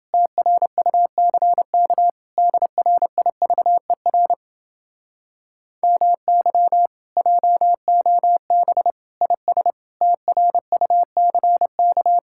Morse code, heard as a keyed tone, about 30 wpm, 700 Hz. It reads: TRUCK DRIVER MY JOB IS TRUCK